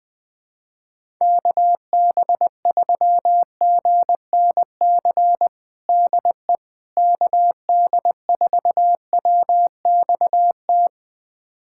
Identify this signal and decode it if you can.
Morse code, heard as a keyed tone, about 21 wpm, 700 Hz. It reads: KB3GNC DE KD4WXT
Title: USA Callsign DE Callsign - 6 Characters - 20wpm